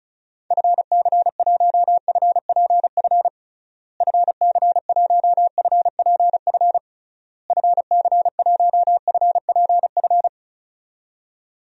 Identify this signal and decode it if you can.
Morse code, heard as a keyed tone, about 34 wpm, 700 Hz. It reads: FC1FPF FC1FPF FC1FPF
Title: DXCC-100 Call Signs - 6 Characters - Repeated 3x - 35wpm